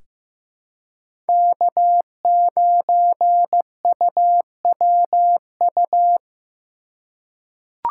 Morse code, heard as a keyed tone, about 15 wpm, 700 Hz. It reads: K9UWU
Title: USA Call Signs - 5 Characters - 2x Character Spacing - 15wpm